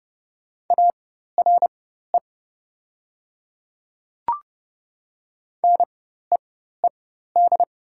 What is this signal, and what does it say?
Morse code, heard as a keyed tone, about 29 wpm, spaced as for 14 wpm, 700 Hz. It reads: ARE NEED
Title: 4x Character Spacing - Top 200 Words - 30wpm